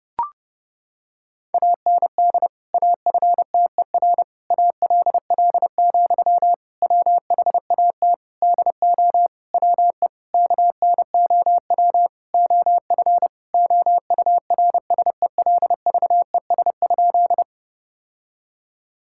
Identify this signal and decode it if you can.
Morse code, heard as a keyed tone, about 30 wpm, 700 Hz. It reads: AND AFTER ALL, WHAT DO WE KNOW OF OURSELVES?
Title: Sentences from Top 1600 Words 30wpm